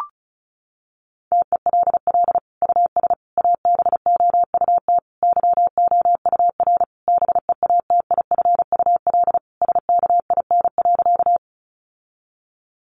Morse code, heard as a keyed tone, about 34 wpm, 700 Hz. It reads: TELL US ABOUT YOUR BEATIFUL SKIN.
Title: Sentences from Top 1300 Words 35wpm